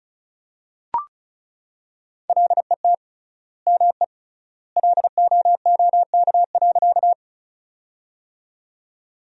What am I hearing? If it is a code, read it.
Morse code, heard as a keyed tone, about 36 wpm, 700 Hz. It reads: LET ME LOOK.
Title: 3x Word-Spacing - Sentences from Top 300 Words - 35wpm